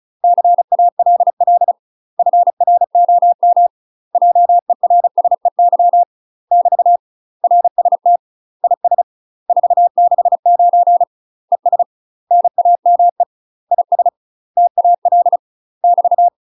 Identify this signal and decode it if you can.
Morse code, heard as a keyed tone, about 34 wpm, 700 Hz. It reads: CALL FROM JERSEY <BT> RST IS 469 ES NAME IS TAL <BT>
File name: UK MASTER QSOs TRAINING / 35WPM / UK MASTER QSOs 0003